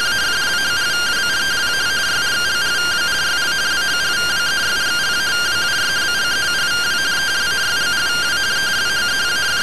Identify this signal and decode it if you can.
Two-tone teleprinter signal, about 170 Hz shift.